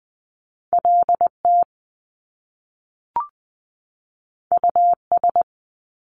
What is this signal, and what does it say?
Morse code, heard as a keyed tone, about 20 wpm, 700 Hz. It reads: L T U S
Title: Top Two Letter Combinations in Words 20wpm